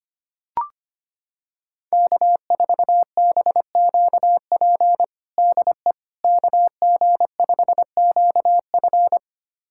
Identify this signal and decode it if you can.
Morse code, heard as a keyed tone, about 25 wpm, 700 Hz. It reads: K4BQP DE KG5QF
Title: USA Callsign DE Callsign 25wpm